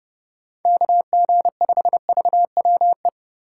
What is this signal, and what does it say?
Morse code, heard as a keyed tone, about 28 wpm, 700 Hz. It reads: KG5VWE